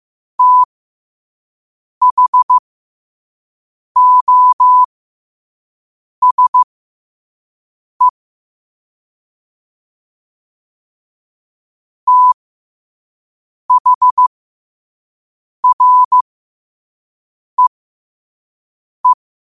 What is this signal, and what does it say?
Morse code, heard as a keyed tone, about 15 wpm, spaced as for 5 wpm, 1000 Hz. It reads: THOSE THREE
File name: qotdfiles / 5wpm